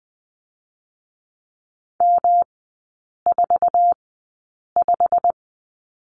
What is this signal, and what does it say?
Morse code, heard as a keyed tone, about 19 wpm, spaced as for 8 wpm, 700 Hz. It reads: M45
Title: 121 Lesson 20e - Number 3 - Three Characters 20wpm